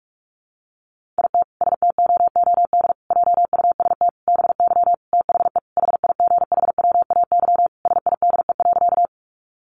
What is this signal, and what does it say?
Morse code, heard as a keyed tone, about 44 wpm, 700 Hz. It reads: IT STOOD JUST BY THE HIGHWAY SIDE.
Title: Sentences from Top 1400 Words 45wpm